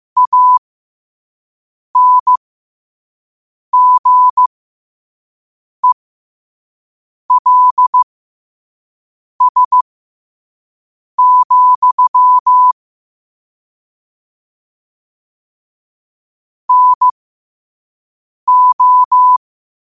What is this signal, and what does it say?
Morse code, heard as a keyed tone, about 15 wpm, spaced as for 5 wpm, 1000 Hz. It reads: ANGELS, NO
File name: qotdfiles / 5wpm